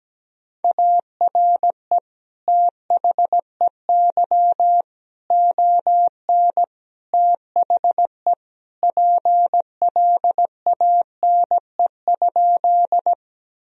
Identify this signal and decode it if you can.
Morse code, heard as a keyed tone, about 17 wpm, 700 Hz. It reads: ARE THEY ON THE PLANE?
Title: Sentences from Top 1000 Words - Review Part 3 - 17wpm